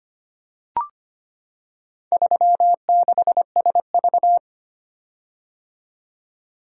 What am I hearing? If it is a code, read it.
Morse code, heard as a keyed tone, about 25 wpm, 690 Hz. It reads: 36SV
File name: practice / 25wpm / Four Letter-Number Combinations - 25wpm